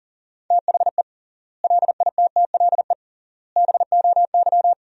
Morse code, heard as a keyed tone, about 37 wpm, 700 Hz. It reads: THE LITTLE BOY
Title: 3x Word-Spacing - Sentences from Top 500 Words - 40wpm